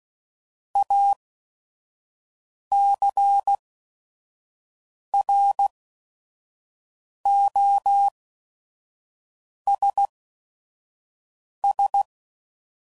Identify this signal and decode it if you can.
Morse code, heard as a keyed tone, about 16 wpm, spaced as for 5 wpm, 800 Hz. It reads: ACROSS
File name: CW Code Groups @5WPM / Technical Subjects / Reactance